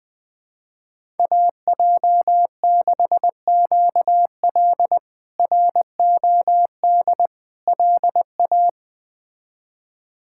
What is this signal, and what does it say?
Morse code, heard as a keyed tone, about 21 wpm, 700 Hz. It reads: AJ6QL ROD LA
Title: K1USN SST QSO Practice - 20wpm